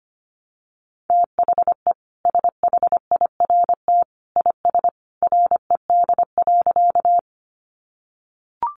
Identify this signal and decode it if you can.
Morse code, heard as a keyed tone, about 25 wpm, 700 Hz. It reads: THE SHIRT IS RED.
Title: Additional Easy Phrases 25wpm